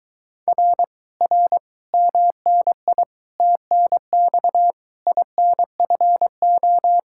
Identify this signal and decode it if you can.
Morse code, heard as a keyed tone, about 23 wpm, 700 Hz. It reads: R R MNI TNX INFO